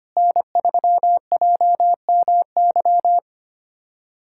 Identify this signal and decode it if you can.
Morse code, heard as a keyed tone, about 25 wpm, 700 Hz. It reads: N3JMY
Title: USA Callsign DE Callsign 25wpm